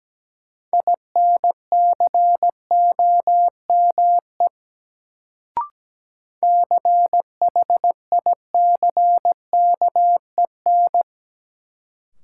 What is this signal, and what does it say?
Morse code, heard as a keyed tone, about 17 wpm, 700 Hz. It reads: INCOME CHICKEN
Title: Sentences from Top 1000 Words - Review Part 5 - New Word Drill - 17wpm